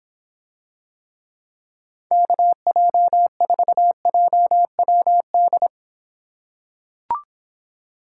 Morse code, heard as a keyed tone, about 26 wpm, 700 Hz. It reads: KJ4JWD